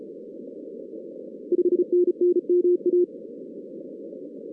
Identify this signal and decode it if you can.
Morse code, heard as a keyed tone, about 36 wpm, 360 Hz. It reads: 5NN MA
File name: 2016 arrl dx cw / 2016-ARRL-DX-CW-K5TR- - 0-46